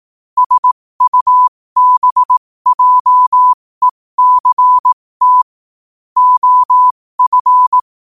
Morse code, heard as a keyed tone, about 18 wpm, spaced as for 15 wpm, 1000 Hz. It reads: SUBJECT OF
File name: qotdfiles / 15wpm